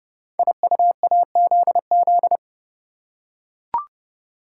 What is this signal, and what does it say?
Morse code, heard as a keyed tone, about 29 wpm, 700 Hz. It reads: IUAZZ
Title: Code Groups - 5 Letters - 30wpm